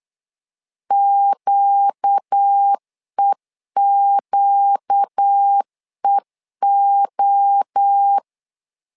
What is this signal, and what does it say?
Morse code, heard as a keyed tone, about 8 wpm, 800 Hz. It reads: QEQEO
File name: LEZIONI / 4 Q 8wpm 2732